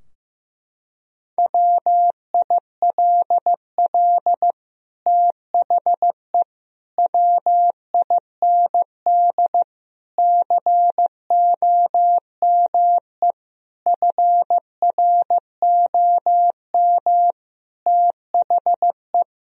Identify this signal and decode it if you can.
Morse code, heard as a keyed tone, about 15 wpm, 700 Hz. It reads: WILL THE WIND COME FROM THE
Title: Sentences from Top 1000 Words - Review Part 2 - 15wpm